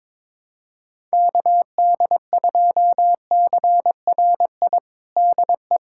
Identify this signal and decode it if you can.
Morse code, heard as a keyed tone, about 22 wpm, 700 Hz. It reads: KD2CRI DE
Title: USA Callsign DE Callsign 22wpm